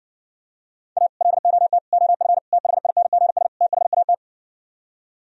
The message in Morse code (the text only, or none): A LOT OF THINGS THAT